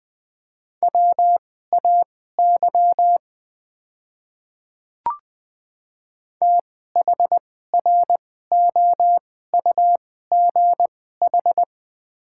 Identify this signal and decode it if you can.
Morse code, heard as a keyed tone, about 20 wpm, spaced as for 14 wpm, 690 Hz. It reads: WAY THROUGH
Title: 2x Character Spacing - Top 300 Words - 20wpm